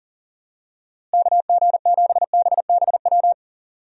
Morse code, heard as a keyed tone, about 40 wpm, 700 Hz. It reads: KG7BBW
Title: USA Call Signs - 6 Characters - Repeated 3x - 40wpm